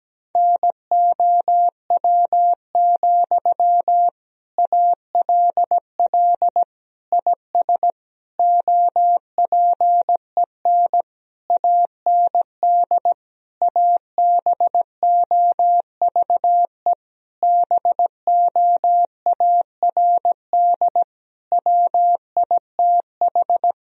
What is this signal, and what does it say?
Morse code, heard as a keyed tone, about 17 wpm, 700 Hz. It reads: NOW, ALL IS OPEN AND ABOVE BOARD WITH
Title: Sentences from Top 1000 Words - Review Part 4 - 17wpm